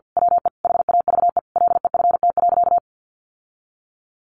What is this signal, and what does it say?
Morse code, heard as a keyed tone, about 50 wpm, 700 Hz. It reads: WE HAVE LEFT.
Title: Sentences from Top 1500 Words 50wpm